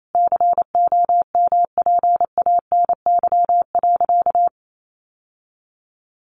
Morse code, heard as a keyed tone, about 28 wpm, 700 Hz. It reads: COMPANY.
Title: Sentences from Top 1100 Words 28wpm